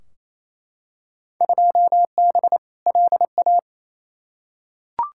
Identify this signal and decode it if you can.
Morse code, heard as a keyed tone, about 28 wpm, 700 Hz. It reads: 2B LA
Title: Field Day Exchanges Encore - 28wpm